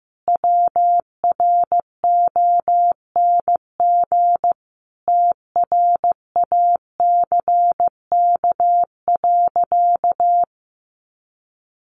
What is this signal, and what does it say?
Morse code, heard as a keyed tone, about 15 wpm, 700 Hz. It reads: WRONG TRACK.
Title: Sentences from Top 1100 Words 15wpm